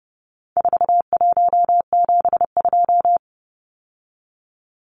Morse code, heard as a keyed tone, about 30 wpm, 700 Hz. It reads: 4172
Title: Numbers - 4-Digit 30wpm